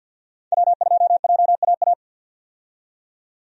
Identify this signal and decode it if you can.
Morse code, heard as a keyed tone, about 51 wpm, 700 Hz. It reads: W2JAA
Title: USA Call Signs - 5 Characters - 50wpm